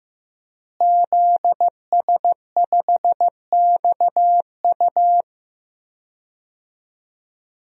Morse code, heard as a keyed tone, about 16 wpm, 700 Hz. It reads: ZS5XU
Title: DXCC-100 Call Signs - 5 Characters - Repeated 3x - 15wpm